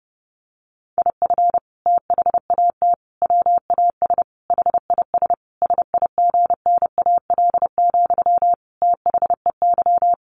Morse code, heard as a keyed tone, about 30 wpm, 700 Hz. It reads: IF THAT WAS HIS SIGNAL, THEY
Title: Sentences from Top 1400 Words 30wpm